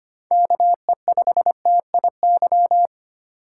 Morse code, heard as a keyed tone, about 33 wpm, 700 Hz. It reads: KE5TIY